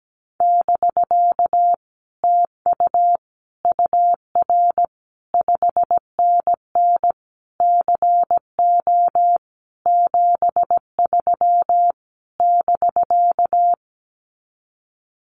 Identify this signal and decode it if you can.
Morse code, heard as a keyed tone, about 17 wpm, 700 Hz. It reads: <BK> TU UR 5NN CO 73 <BK>
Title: Common QSO Phrases Encore 17wpm